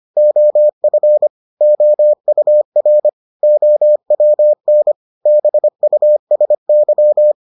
Morse code, heard as a keyed tone, about 25 wpm, 600 Hz. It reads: OF OUR OWN BUSY